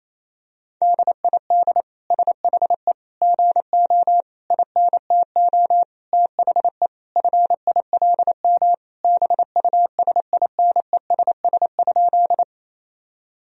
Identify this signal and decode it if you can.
Morse code, heard as a keyed tone, about 27 wpm, 700 Hz. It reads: DID SHE GO INTO THE FILM BUSINESS?
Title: Sentences from Top 1000 Words - Review Part 3 - 28wpm